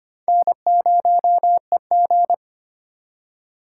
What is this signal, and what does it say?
Morse code, heard as a keyed tone, about 33 wpm, 700 Hz. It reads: N0EG